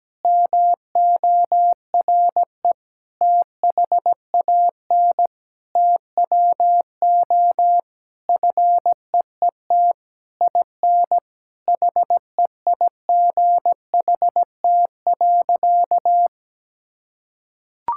Morse code, heard as a keyed tone, about 17 wpm, 700 Hz. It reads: MORE THAN TWO FEET IN HEIGHT.
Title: Sentences from Top 1600 Words 17wpm